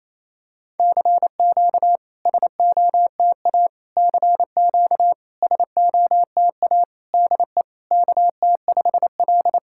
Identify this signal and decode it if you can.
Morse code, heard as a keyed tone, about 28 wpm, 700 Hz. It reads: CQ SOTA CQ SOTA DE KT5L
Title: SOTA QSO Practice 28wpm